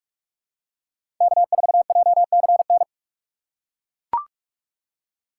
Morse code, heard as a keyed tone, about 42 wpm, 700 Hz. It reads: K4JCN
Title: USA Call Signs - 5 Characters - 6x Character Spacing - 45wpm